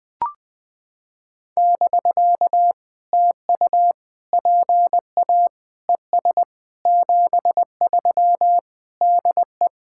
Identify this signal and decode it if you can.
Morse code, heard as a keyed tone, about 19 wpm, 700 Hz. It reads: <BK> TU PA ES 73 DE
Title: POTA QSO Practice - 20wpm